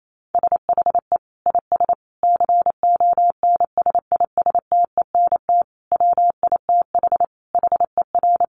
Morse code, heard as a keyed tone, about 28 wpm, 700 Hz. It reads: SHE IS CONSISTENT WITH HER
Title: Sentences from Top 1500 Words 28wpm